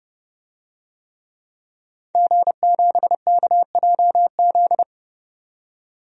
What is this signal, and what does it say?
Morse code, heard as a keyed tone, about 30 wpm, 700 Hz. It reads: G7KJZ
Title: DXCC-100 Call Signs - 5 Characters - 30wpm